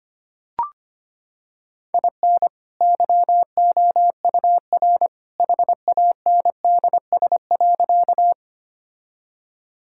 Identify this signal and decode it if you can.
Morse code, heard as a keyed tone, about 25 wpm, 700 Hz. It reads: IN YOUR HANDS.
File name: practice / 25wpm / Sentences with 2-Word Phrases - 25wpm